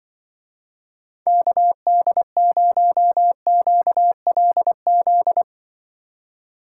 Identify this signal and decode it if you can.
Morse code, heard as a keyed tone, about 24 wpm, 700 Hz. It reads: KD0QLZ
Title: Speed Racing - USA 2x3 Call Signs 20wpm